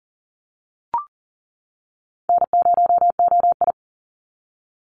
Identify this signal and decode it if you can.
Morse code, heard as a keyed tone, about 40 wpm, 700 Hz. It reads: N0OI
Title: NA SOTA Chaser Callsigns 40wpm